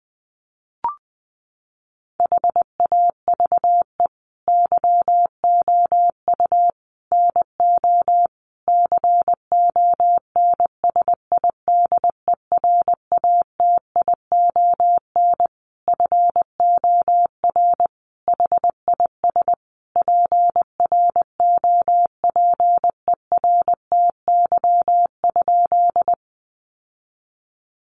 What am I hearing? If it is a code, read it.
Morse code, heard as a keyed tone, about 20 wpm, 700 Hz. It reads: HAVE YOU NO CONSIDERATION FOR HIS PROPERTY?
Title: Sentences from Top 1500 Words 20wpm